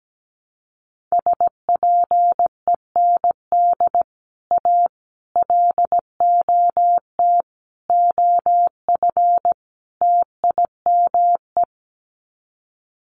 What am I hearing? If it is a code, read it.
Morse code, heard as a keyed tone, about 17 wpm, 700 Hz. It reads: SPEND A LOT OF TIME
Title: Sets of 5 Words 17wpm